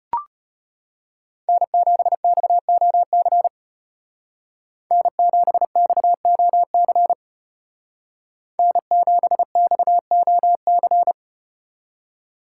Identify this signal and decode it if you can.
Morse code, heard as a keyed tone, about 34 wpm, 700 Hz. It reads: N7XOC N7XOC N7XOC